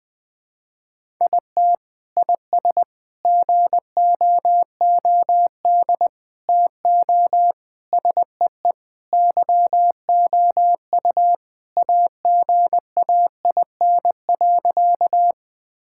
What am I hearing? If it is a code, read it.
Morse code, heard as a keyed tone, about 20 wpm, 700 Hz. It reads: IT IS GOOD TO SEE YOU AGAIN.
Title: Sentences with 4-Word Phrases - Spoken Beginning - 20wpm